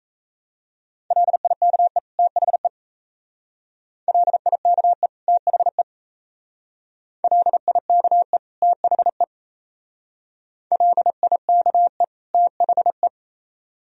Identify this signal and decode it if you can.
Morse code, heard as a keyed tone, about 33 wpm, 690 Hz. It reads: LIKE THE LIKE THE LIKE THE LIKE THE
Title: Speed-Racing - Sets of 2 Words 28wpm